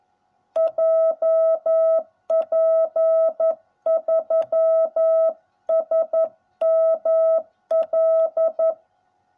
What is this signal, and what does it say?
Morse code, heard as a keyed tone, about 11 wpm, 660 Hz. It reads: JP3SML